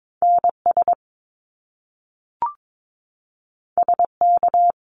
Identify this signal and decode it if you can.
Morse code, heard as a keyed tone, about 22 wpm, 700 Hz. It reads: NS SK
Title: Canadian Provinces and Territories 22wpm